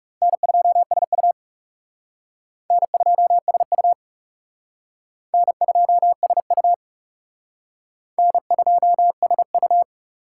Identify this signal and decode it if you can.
Morse code, heard as a keyed tone, about 38 wpm, 700 Hz. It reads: N2SU N2SU N2SU N2SU
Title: Speed Racing - USA 1x2 Call Signs 30wpm